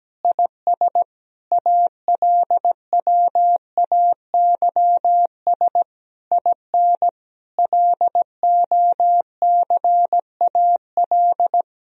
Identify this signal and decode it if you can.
Morse code, heard as a keyed tone, about 17 wpm, 700 Hz. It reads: IS ALWAYS IN LOCAL
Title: Sentences from Top 1000 Words - Review Part 3 - 17wpm